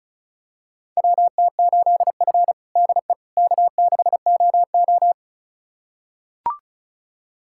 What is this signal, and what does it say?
Morse code, heard as a keyed tone, about 34 wpm, 700 Hz. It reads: WT8F DE K6OO
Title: USA Callsign DE Callsign - 4 Characters - 35wpm